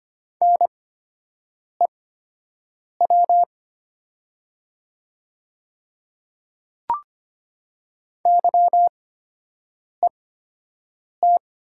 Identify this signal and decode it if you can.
Morse code, heard as a keyed tone, about 25 wpm, spaced as for 7 wpm, 700 Hz. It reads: NEW YET
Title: 8x Character Spacing - Top 500 Words - 25wpm